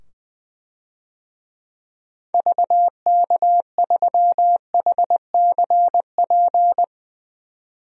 Morse code, heard as a keyed tone, about 21 wpm, 700 Hz. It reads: VK3HCP
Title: DXCC-100 Call Signs - 6 Characters - 20wpm